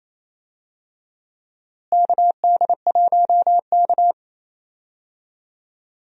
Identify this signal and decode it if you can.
Morse code, heard as a keyed tone, about 28 wpm, 700 Hz. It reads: KD1K
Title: USA Call Signs - 4 Characters - Repeated 3x - 28wpm